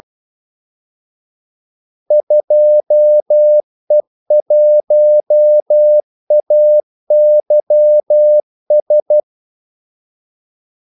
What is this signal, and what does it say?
Morse code, heard as a keyed tone, about 12 wpm, 600 Hz. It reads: 2E1AYS
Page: 2 ekstra ordmellemrum mellem kaldesignalerne